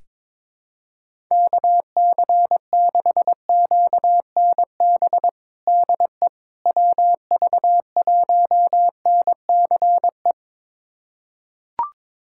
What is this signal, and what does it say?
Morse code, heard as a keyed tone, about 22 wpm, 700 Hz. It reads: KC6QNB DE WV1NCE
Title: USA Callsign DE Callsign - 6 Characters - 22wpm